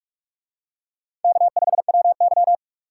Morse code, heard as a keyed tone, about 40 wpm, 700 Hz. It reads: K5WY